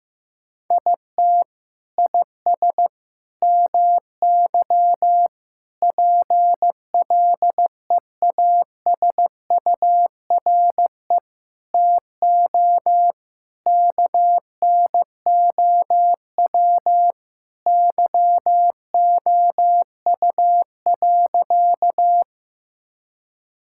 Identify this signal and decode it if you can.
Morse code, heard as a keyed tone, about 15 wpm, 700 Hz. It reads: IT IS MY PLEASURE TO KNOW YOU.
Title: Sentences from Top 1600 Words 15wpm